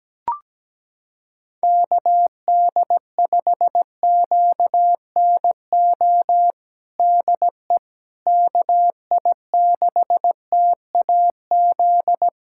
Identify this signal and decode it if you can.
Morse code, heard as a keyed tone, about 17 wpm, 700 Hz. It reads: KD5QNO DE KI6TAZ
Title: USA Callsign DE Callsign 17wpm